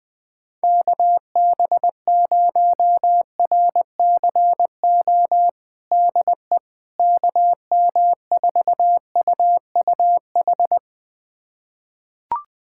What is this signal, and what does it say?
Morse code, heard as a keyed tone, about 20 wpm, 700 Hz. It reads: KB0RCO DE KM4UUH
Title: USA Callsign DE Callsign - 6 Characters - 20wpm